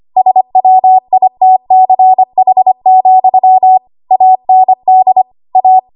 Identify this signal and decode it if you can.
Morse code, heard as a keyed tone, about 25 wpm, 750 Hz. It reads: SWITCH, AND A
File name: Archive / 25 WPM / 25WPM